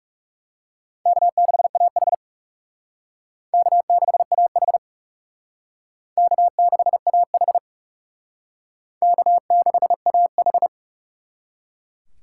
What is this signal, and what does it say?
Morse code, heard as a keyed tone, about 38 wpm, 700 Hz. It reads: K6AH K6AH K6AH K6AH